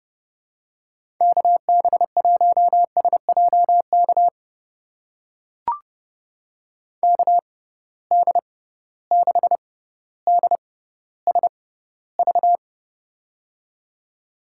Morse code, heard as a keyed tone, about 30 wpm, spaced as for 15 wpm, 700 Hz. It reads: KB1SJK KD6DSV
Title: USA Call Signs - 6 Characters - 6x Character Spacing - 30wpm